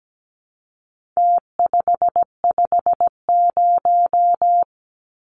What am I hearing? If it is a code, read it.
Morse code, heard as a keyed tone, about 17 wpm, 700 Hz. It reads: T550
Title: Numbers - Contest Style - 4-Digit 17wpm